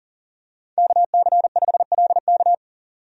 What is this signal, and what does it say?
Morse code, heard as a keyed tone, about 40 wpm, 700 Hz. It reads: KC5LK
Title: USA Call Signs - 5 Characters - 40wpm